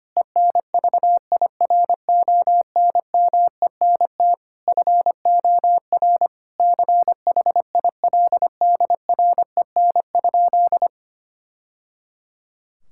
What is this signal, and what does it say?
Morse code, heard as a keyed tone, about 25 wpm, 700 Hz. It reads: ENVIRONMENT FOR CHILDREN?
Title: Sentences from Top 1000 Words - Review Part 5 - 25wpm